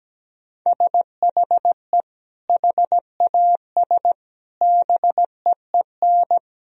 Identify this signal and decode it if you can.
Morse code, heard as a keyed tone, about 17 wpm, 700 Hz. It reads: SHE HAS BEEN
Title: Sentences from Top 1000 Words - Review Part 4 - 17wpm